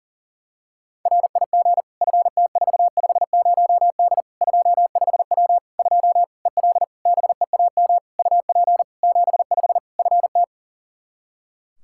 Morse code, heard as a keyed tone, about 40 wpm, 700 Hz. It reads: RIG FT450D 25W 2 EL BEAM UP 75 FT
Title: Practice QSO - Exchange 2 - Info - 40wpm